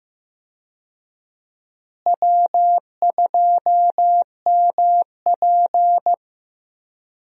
Morse code, heard as a keyed tone, about 15 wpm, 700 Hz. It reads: W2MP
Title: USA Call Signs - 4 Characters - Repeated 3x - 15wpm